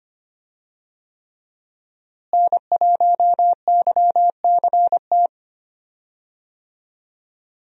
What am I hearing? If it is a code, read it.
Morse code, heard as a keyed tone, about 25 wpm, 700 Hz. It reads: N1YCT
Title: USA Call Signs - 5 Characters - Repeated 3x - 25wpm